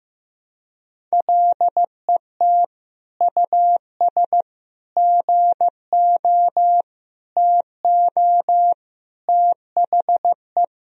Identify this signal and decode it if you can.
Morse code, heard as a keyed tone, about 15 wpm, 700 Hz. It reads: LET US GO TO THE
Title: Sentences from Top 1000 Words - Review Part 1 - 15wpm